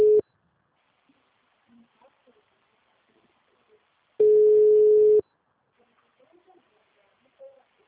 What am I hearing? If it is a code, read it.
Ringback tone (Europe, 425 Hz).
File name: out- -2001- - - .2